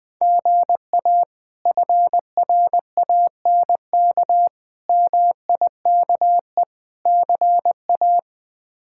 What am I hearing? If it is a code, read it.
Morse code, heard as a keyed tone, about 20 wpm, 700 Hz. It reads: GA FRANK MIKE CA
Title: K1USN SST QSO Practice - 20wpm